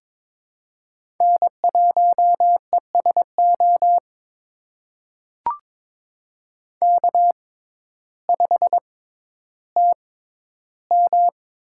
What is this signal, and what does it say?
Morse code, heard as a keyed tone, about 21 wpm, spaced as for 12 wpm, 700 Hz. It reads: N1ESO K5TM
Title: USA Call Signs - 5 Characters - 6x Character Spacing - 22wpm